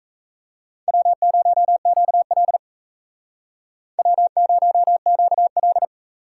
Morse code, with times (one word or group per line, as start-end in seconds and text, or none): 0.9-2.6 W0QL
4.0-5.8 W0QL